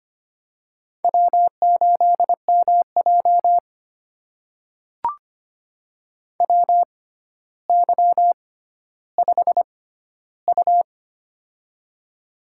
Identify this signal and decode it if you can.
Morse code, heard as a keyed tone, about 25 wpm, spaced as for 13 wpm, 700 Hz. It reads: W8MJ WY5U
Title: USA Call Signs - 4 Characters - 6x Character Spacing - 25wpm